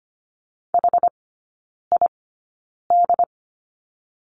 Morse code, heard as a keyed tone, about 25 wpm, spaced as for 9 wpm, 690 Hz. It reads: HID
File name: Lesson 14e - Number 2 - Three Characters 25wpm